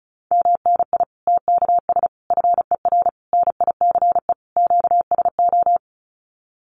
Morse code, heard as a keyed tone, about 34 wpm, 700 Hz. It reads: MNI TKS FER NICE QSO
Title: Common QSO Phrases 35wpm